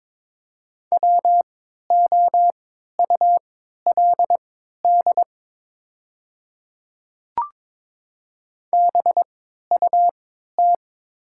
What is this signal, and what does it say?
Morse code, heard as a keyed tone, about 22 wpm, spaced as for 12 wpm, 690 Hz. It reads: WOULD BUT
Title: 3x Character Spacing - Top 200 Words - 22wpm